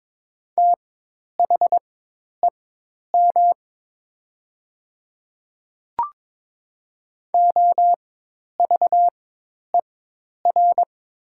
Morse code, heard as a keyed tone, about 22 wpm, spaced as for 10 wpm, 700 Hz. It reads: THEM OVER
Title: 4x Character Spacing - Top 100 Words - 22wpm